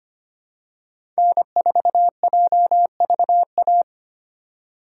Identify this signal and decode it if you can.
Morse code, heard as a keyed tone, about 25 wpm, 700 Hz. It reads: N4JVA
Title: USA Call Signs - 5 Characters - 2x Character Spacing - 25wpm